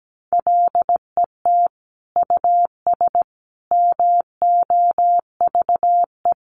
Morse code, heard as a keyed tone, about 17 wpm, 700 Hz. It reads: LET US MOVE
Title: Sentences from Top 1500 Words 17wpm